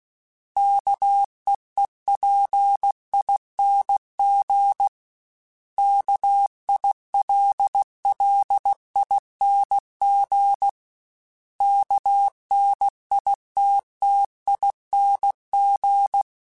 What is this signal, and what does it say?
Morse code, heard as a keyed tone, about 16 wpm, 800 Hz. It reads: KEEPING KILLING KNITTING
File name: Special INGs / Special INGs-mp3s / Sngl ing @15